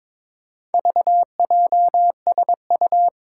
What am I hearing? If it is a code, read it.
Morse code, heard as a keyed tone, about 23 wpm, 700 Hz. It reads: VJSU
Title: Four Challenging Letter-Number Combinations 22wpm